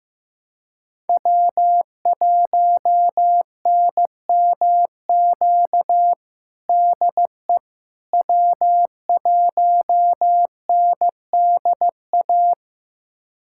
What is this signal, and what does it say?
Morse code, heard as a keyed tone, about 15 wpm, 700 Hz. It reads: W1NMQ DE W1NDA
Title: USA Callsign DE Callsign 15wpm